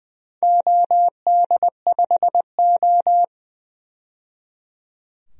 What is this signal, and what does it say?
Morse code, heard as a keyed tone, about 20 wpm, 700 Hz. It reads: OD5O
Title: DXCC-100 Call Signs - 4 Characters - 20wpm